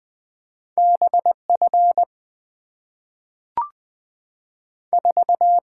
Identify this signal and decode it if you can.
Morse code, heard as a keyed tone, about 20 wpm, 700 Hz. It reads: BF 4
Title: Two Letter-Number Combinations - 20wpm